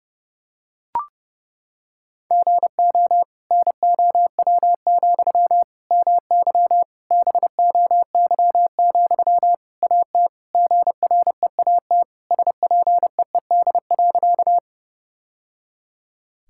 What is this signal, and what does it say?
Morse code, heard as a keyed tone, about 30 wpm, 700 Hz. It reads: GO NOW, MY BOY, AT GREAT SPEED.
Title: Sentences from Top 1000 Words - Review Part 5 - 30wpm